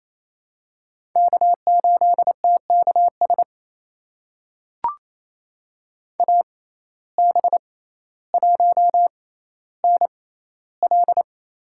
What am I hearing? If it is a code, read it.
Morse code, heard as a keyed tone, about 28 wpm, spaced as for 14 wpm, 700 Hz. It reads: K8TKS AB1NL
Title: USA Call Signs - 5 Characters - 6x Character Spacing - 28wpm